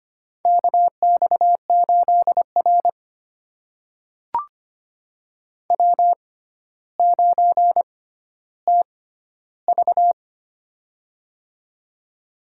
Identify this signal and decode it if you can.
Morse code, heard as a keyed tone, about 25 wpm, spaced as for 13 wpm, 700 Hz. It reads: KX8R W9TV
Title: USA Call Signs - 4 Characters - 6x Character Spacing - 25wpm